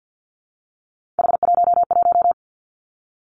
Morse code, heard as a keyed tone, about 50 wpm, 700 Hz. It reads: H11
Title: Three Challenging Letter-Number Combinations 50wpm